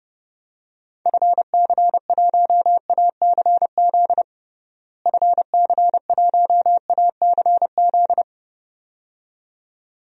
Morse code, heard as a keyed tone, about 32 wpm, 700 Hz. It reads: FC1ACZ FC1ACZ
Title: DXCC-100 Call Signs - 6 Characters - Repeated 2x - 30wpm